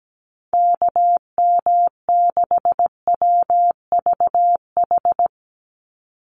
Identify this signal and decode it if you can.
Morse code, heard as a keyed tone, about 17 wpm, 700 Hz. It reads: KM6WVH
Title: USA Call Signs 17wpm